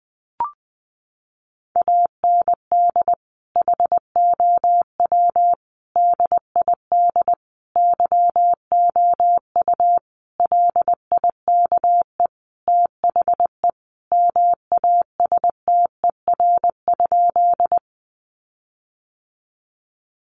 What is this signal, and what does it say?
Morse code, heard as a keyed tone, about 19 wpm, 700 Hz. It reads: AND HOW DID YOU LIKE THE MASTER?
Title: Sentences from Top 1300 Words 20wpm